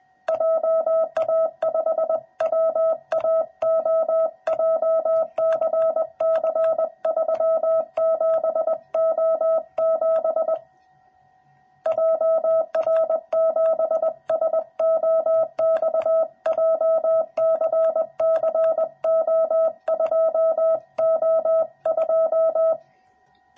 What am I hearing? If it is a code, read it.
Morse code, heard as a keyed tone, about 20 wpm, 650 Hz. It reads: JA5WAO JCC37O7 JR7SOX JCCO2O2
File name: jh8cbh / 2021cw / cw06b